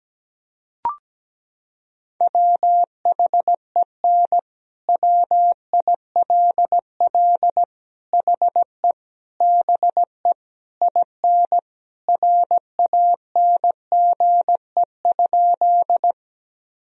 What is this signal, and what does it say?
Morse code, heard as a keyed tone, about 17 wpm, 700 Hz. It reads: WHEN WILL HE BE IN RANGE?
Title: Sentences from Top 1000 Words - Review Part 4 - 17wpm